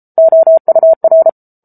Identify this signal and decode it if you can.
Morse code, heard as a keyed tone, about 34 wpm, 650 Hz. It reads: OUR
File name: Book Two / 07-Chapter 7 - 33 WPM